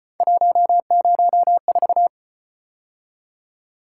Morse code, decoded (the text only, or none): 104